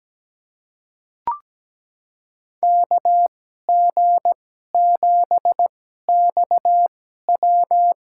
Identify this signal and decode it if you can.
Morse code, heard as a keyed tone, about 17 wpm, spaced as for 12 wpm, 700 Hz. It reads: KG7XW
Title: USA Call Signs - 5 Characters - 2x Character Spacing - 17wpm